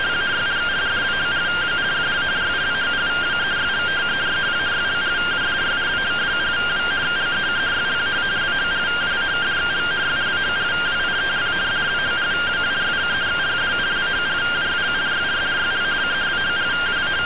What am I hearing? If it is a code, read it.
Two-tone teleprinter signal, about 170 Hz shift.